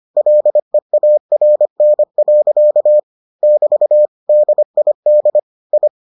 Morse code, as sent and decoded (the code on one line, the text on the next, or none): .-.. .- .-. -. .-.-.- / -...- / -.. .. -.. / ..
LEARN. <BT> DID I